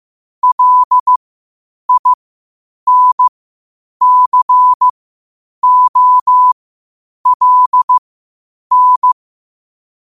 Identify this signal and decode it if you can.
Morse code, heard as a keyed tone, about 15 wpm, spaced as for 8 wpm, 1000 Hz. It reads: LINCOLN
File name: qotdfiles / 8wpm